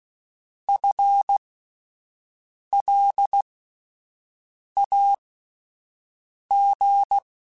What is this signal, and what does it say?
Morse code, heard as a keyed tone, about 16 wpm, spaced as for 5 wpm, 800 Hz. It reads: FLAG
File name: Alphabetical / Alphabetical mp3s / Fs @5